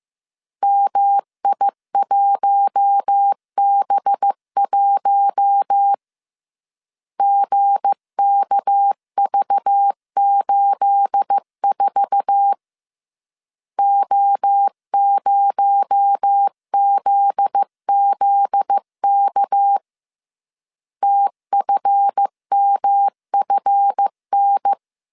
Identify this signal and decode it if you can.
Morse code, heard as a keyed tone, about 15 wpm, 800 Hz. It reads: MI1B1 GKV84 O0ZZK TFMFN
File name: CW / DET / 1 9 1500C S A 14wpm 3222